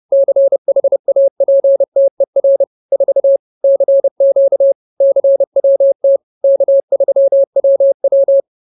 Morse code, heard as a keyed tone, about 30 wpm, 550 Hz. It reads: CHAPTER 4 CQ CWT K3WW
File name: CWT-Chapter-04